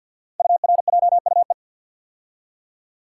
Morse code, heard as a keyed tone, about 49 wpm, 700 Hz. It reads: URJUE
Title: Code Groups - 5 Letters - 50wpm